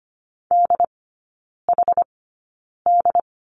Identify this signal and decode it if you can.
Morse code, heard as a keyed tone, about 25 wpm, spaced as for 9 wpm, 700 Hz. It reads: DHD
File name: Lesson 15e - Number 5 - Three Characters 25wpm